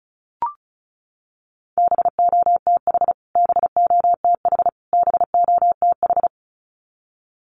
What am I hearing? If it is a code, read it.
Morse code, heard as a keyed tone, about 34 wpm, 700 Hz. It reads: BOTH BOTH BOTH
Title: Top 500 Words Repeated 35wpm